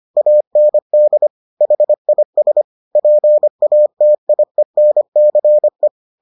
Morse code, heard as a keyed tone, about 25 wpm, 600 Hz. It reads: AND HIS PATIENCE